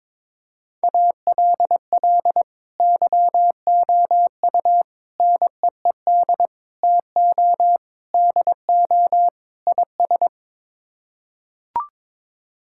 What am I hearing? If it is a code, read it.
Morse code, heard as a keyed tone, about 21 wpm, 700 Hz. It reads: ALL YOU NEED TO DO IS
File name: practice / 22wpm / Sets of 6 Words - 22wpm